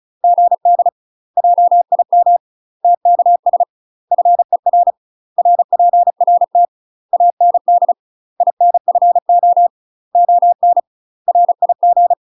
Morse code, heard as a keyed tone, about 36 wpm, 700 Hz. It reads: GD JIM TKS FER RPRT AND INFO ON RIG
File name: UK MASTER QSOs TRAINING/35WPM